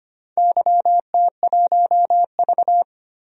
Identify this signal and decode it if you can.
Morse code, heard as a keyed tone, about 25 wpm, 700 Hz. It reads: YT1V
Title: DXCC-100 Call Signs - 4 Characters - 25wpm